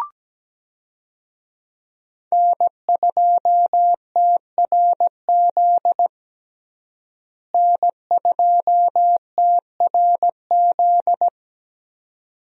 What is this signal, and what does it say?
Morse code, heard as a keyed tone, about 17 wpm, 700 Hz. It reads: N2TRZ N2TRZ
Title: USA Call Signs - 5 Characters - Repeated 2x - 17wpm